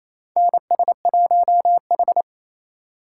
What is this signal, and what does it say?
Morse code, heard as a keyed tone, about 27 wpm, 700 Hz. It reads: NS1H
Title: USA Call Signs - 4 Characters - 6x Character Spacing - 28wpm